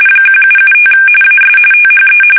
RTTY, about 45.45 baud. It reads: 1775 AS THE
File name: fsk850 / usnavy1